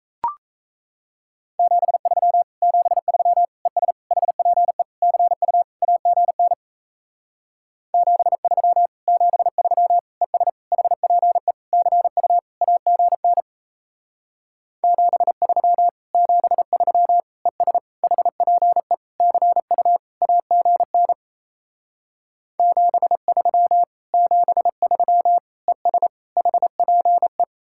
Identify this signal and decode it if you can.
Morse code, heard as a keyed tone, about 35 wpm, 700 Hz. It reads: 73 73 ES HPE CU AGN 73 73 ES HPE CU AGN 73 73 ES HPE CU AGN 73 73 ES HPE
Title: Speed-Racing Common QSO Phrases Encore - 28wpm